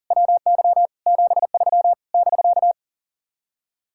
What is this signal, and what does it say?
Morse code, heard as a keyed tone, about 50 wpm, 700 Hz. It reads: WY 73 <BK>
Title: Speed-Racing Common QSO Phrases Encore - 40wpm